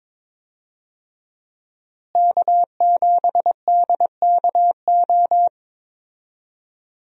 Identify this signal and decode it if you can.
Morse code, heard as a keyed tone, about 22 wpm, 700 Hz. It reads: K7DKO
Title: USA Call Signs - 5 Characters - Repeated 2x - 22wpm